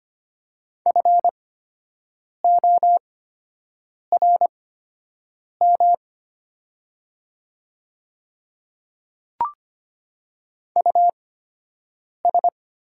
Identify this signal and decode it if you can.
Morse code, heard as a keyed tone, about 25 wpm, spaced as for 7 wpm, 700 Hz. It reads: FORM US
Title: 8x Character Spacing - Top 500 Words - 25wpm